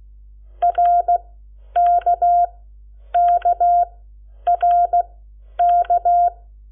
Morse code, heard as a keyed tone, about 16 wpm, spaced as for 9 wpm, 700 Hz. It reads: RKKRK
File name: Lecciones / CLASE Nº7